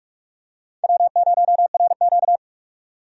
Morse code, heard as a keyed tone, about 43 wpm, 700 Hz. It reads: W0RQ
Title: USA Call Signs - 4 Characters - 45wpm